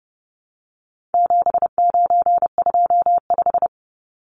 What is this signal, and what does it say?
Morse code, heard as a keyed tone, about 30 wpm, 700 Hz. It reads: 7925
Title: Numbers - 4-Digit 30wpm